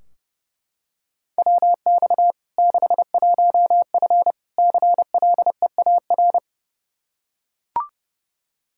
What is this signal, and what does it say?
Morse code, heard as a keyed tone, about 30 wpm, 700 Hz. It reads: WX 61F CLEAR
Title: K1USN SST QSO Practice 30wpm